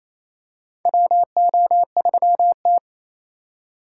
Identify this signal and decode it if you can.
Morse code, heard as a keyed tone, about 27 wpm, 700 Hz. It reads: WO3T
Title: USA Call Signs - 4 Characters - 2x Character Spacing - 28wpm